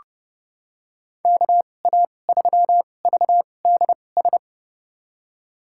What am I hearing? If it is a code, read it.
Morse code, heard as a keyed tone, about 30 wpm, spaced as for 22 wpm, 700 Hz. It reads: KA3VDS
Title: USA Call Signs - 6 Characters - 2x Character Spacing - 30wpm